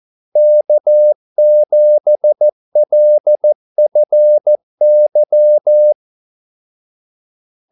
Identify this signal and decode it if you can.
Morse code, heard as a keyed tone, about 14 wpm, 600 Hz. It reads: K7LFY
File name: Callsigns-14wpm